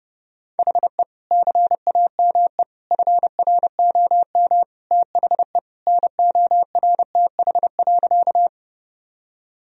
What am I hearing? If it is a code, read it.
Morse code, heard as a keyed tone, about 32 wpm, 700 Hz. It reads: HE CAME FROM THE NORTH.
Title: Sentences from Top 1000 Words - Review Part 1 - 30wpm